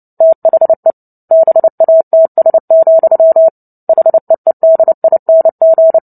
Morse code, heard as a keyed tone, about 30 wpm, 650 Hz. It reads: THE BATS, HEEDING
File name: Book Two / 03-Chapter 3 - 29 WPM